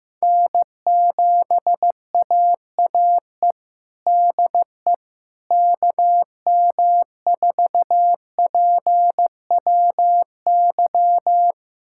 Morse code, heard as a keyed tone, about 15 wpm, 700 Hz. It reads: N7AAE DE KM4PWY
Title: USA Callsign DE Callsign 15wpm